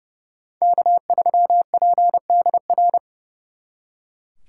Morse code, heard as a keyed tone, about 34 wpm, 700 Hz. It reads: K3PDR
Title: Speed Racing - USA 1x3 Call Signs 30wpm